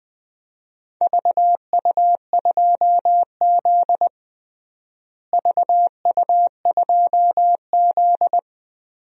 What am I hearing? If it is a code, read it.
Morse code, heard as a keyed tone, about 21 wpm, 700 Hz. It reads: VU2Z VU2Z
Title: DXCC-100 Call Signs - 4 Characters - Repeated 3x - 20wpm